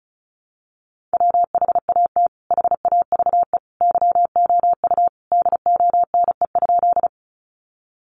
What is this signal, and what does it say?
Morse code, heard as a keyed tone, about 34 wpm, 700 Hz. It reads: WHAT HAVE YOU DONE?
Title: Sentences from Top 500 Words 35wpm